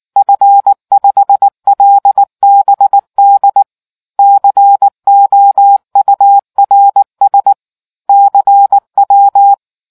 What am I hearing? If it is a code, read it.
Morse code, heard as a keyed tone, about 19 wpm, 800 Hz. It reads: F5LBD COURS CW